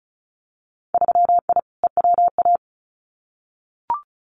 Morse code, heard as a keyed tone, about 34 wpm, 700 Hz. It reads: 3I EWA
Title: Winter Field Day Exchanges Encore 35wpm